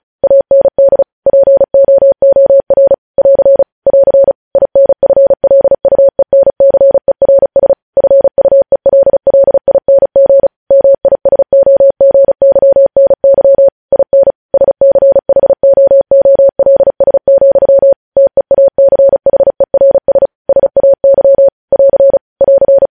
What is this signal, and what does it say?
Morse code, heard as a keyed tone, about 36 wpm, 550 Hz. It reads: AND POOR <AR> <AR> INFLUENCERS FUELLING MISOGYNY IN SCHOOLS, TEACHERS SAY <AR> <AR>
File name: podcast / News headlines-1x-35x35